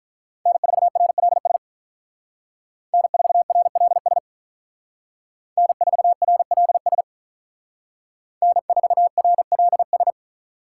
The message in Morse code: N4RLS N4RLS N4RLS N4RLS

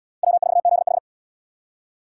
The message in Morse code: W4ZH

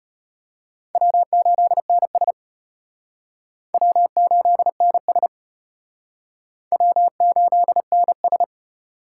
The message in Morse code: W8NS W8NS W8NS

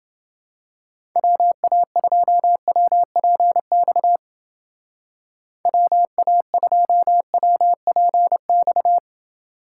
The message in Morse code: WA2WPX WA2WPX

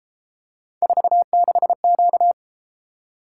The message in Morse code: 46Q

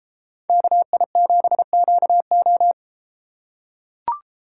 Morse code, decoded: KI7QO

